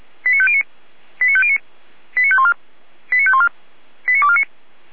Five-tone selective call (ZVEI), heard as quick runs of five tones, tones of about 70 ms. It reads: 89509 89509 89514 89514 89249
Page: SDR/ZVEI